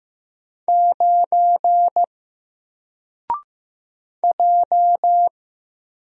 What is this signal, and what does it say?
Morse code, heard as a keyed tone, about 15 wpm, 700 Hz. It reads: 9 J
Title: Single Letter-Number 15wpm